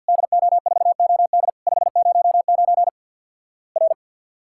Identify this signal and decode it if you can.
Morse code, heard as a keyed tone, about 50 wpm, 700 Hz. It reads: DO4OD 509 R
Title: Worked All Europe DX Contest - QTC Exchange - 50wpm